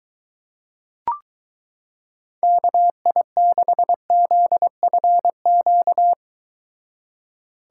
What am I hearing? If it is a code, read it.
Morse code, heard as a keyed tone, about 23 wpm, 700 Hz. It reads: KI6ZFQ